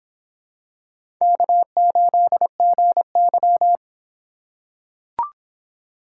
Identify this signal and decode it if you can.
Morse code, heard as a keyed tone, about 24 wpm, 700 Hz. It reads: K8GY